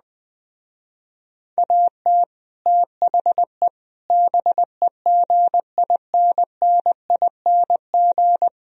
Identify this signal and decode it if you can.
Morse code, heard as a keyed tone, about 20 wpm, 700 Hz. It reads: AT THE BEGINNING